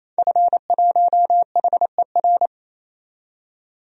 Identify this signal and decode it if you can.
Morse code, heard as a keyed tone, about 28 wpm, 700 Hz. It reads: F1HER
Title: DXCC-100 Call Signs - 5 Characters - 28wpm